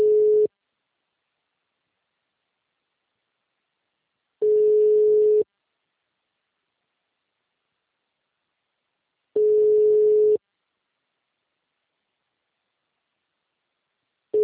Ringback tone (Europe, 425 Hz).